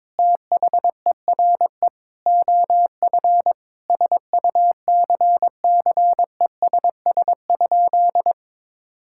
Morse code, heard as a keyed tone, about 22 wpm, 700 Hz. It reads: THERE OF SUCCESS?
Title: Sentences from Top 1000 Words - Review Part 6 - 22wpm